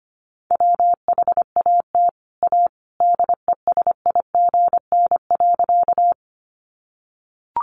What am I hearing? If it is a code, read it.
Morse code, heard as a keyed tone, about 25 wpm, 700 Hz. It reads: WHAT A DESIGN.
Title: Sentences from Top 800 Words 25wpm